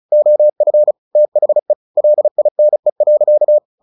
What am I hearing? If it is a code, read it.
Morse code, heard as a keyed tone, about 34 wpm, 600 Hz. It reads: OF THE LINE.
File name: the old man and the sea / the old man and the sea 1 35wpm